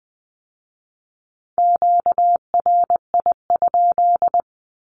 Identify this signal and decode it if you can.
Morse code, heard as a keyed tone, about 19 wpm, 700 Hz. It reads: QRI?
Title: Q-Codes with Spoken Meaning - 20wpm